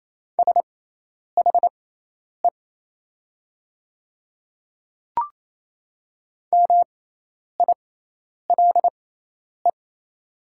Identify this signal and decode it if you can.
Morse code, heard as a keyed tone, about 28 wpm, spaced as for 10 wpm, 700 Hz. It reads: SHE MILE
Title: 6x Character Spacing - Top 500 Words - 28wpm